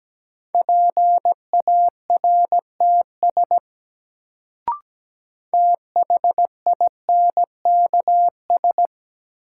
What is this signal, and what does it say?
Morse code, heard as a keyed tone, about 17 wpm, 700 Hz. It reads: PARTS THINKS
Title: Sentences from Top 1000 Words - Review Part 6 - New Word Drill - 17wpm